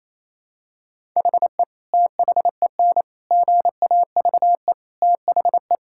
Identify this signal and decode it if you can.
Morse code, heard as a keyed tone, about 28 wpm, 700 Hz. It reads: HE THEN GAVE THE
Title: Sentences from Top 1000 Words - Review Part 6 - 28wpm